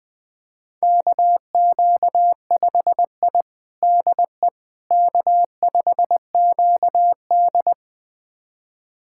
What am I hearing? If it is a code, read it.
Morse code, heard as a keyed tone, about 20 wpm, 700 Hz. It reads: KQ5I DE K5QD
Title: USA Callsign DE Callsign - 4 Characters - 20wpm